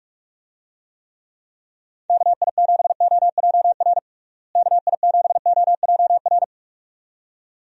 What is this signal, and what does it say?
Morse code, heard as a keyed tone, about 46 wpm, 700 Hz. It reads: KI7OJR KI7OJR
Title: USA Call Signs - 6 Characters - Repeated 2x - 45wpm